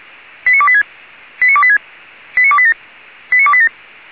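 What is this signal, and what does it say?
Five-tone selective call (ZVEI), heard as quick runs of five tones, tones of about 70 ms. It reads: 89286 89286 89287 89287
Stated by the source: ZVEI five-tone call received by SDR, from feuerwehr-orlamuende.org (Orlamünde fire brigade).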